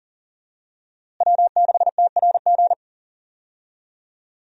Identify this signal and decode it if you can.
Morse code, heard as a keyed tone, about 40 wpm, 700 Hz. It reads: W6TRG
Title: USA Call Signs - 5 Characters - Repeated 2x - 40wpm